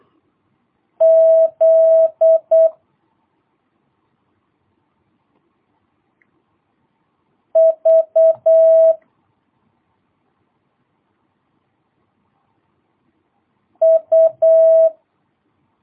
Morse code, heard as a keyed tone, about 8 wpm, 650 Hz. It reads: Z V U
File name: jh8cbh / 2022cw